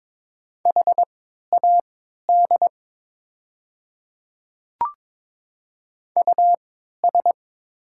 Morse code, heard as a keyed tone, about 22 wpm, spaced as for 12 wpm, 700 Hz. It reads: HAD US
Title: 3x Character Spacing - Top 200 Words - 22wpm